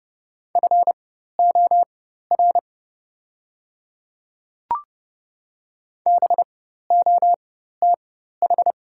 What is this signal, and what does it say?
Morse code, heard as a keyed tone, about 30 wpm, spaced as for 14 wpm, 700 Hz. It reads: FOR BOTH
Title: 4x Character Spacing - Top 300 Words - 30wpm